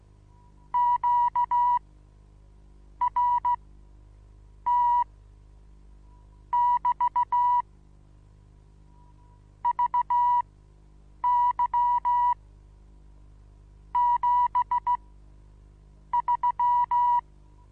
Morse code, heard as a keyed tone, about 15 wpm, spaced as for 6 wpm, 970 Hz. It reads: QRT<BT> VY73
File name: Onlinelehrgang / morsen / DJ4UF Morsen25